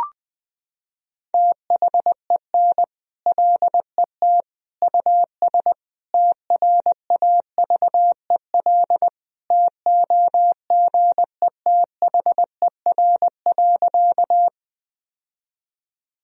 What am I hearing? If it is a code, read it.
Morse code, heard as a keyed tone, about 20 wpm, 700 Hz. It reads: THEN LET US TRAVEL TOGETHER.
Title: Sentences from Top 1000 Words - Review Part 2 - 20wpm